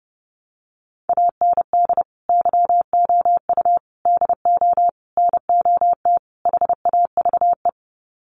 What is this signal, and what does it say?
Morse code, heard as a keyed tone, about 30 wpm, 700 Hz. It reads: AND YOU DO NOT HAVE
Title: Sets of 5 Words Encore 30wpm